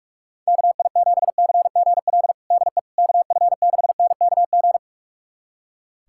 Morse code, heard as a keyed tone, about 44 wpm, 690 Hz. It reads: KI7CGL DE KF6NKG
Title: USA Callsign DE Callsign - 6 Characters - 45wpm